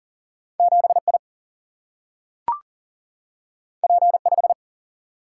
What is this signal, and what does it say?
Morse code, heard as a keyed tone, about 40 wpm, 700 Hz. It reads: 7 I P 5
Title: Single-Letter Single-Number 40wpm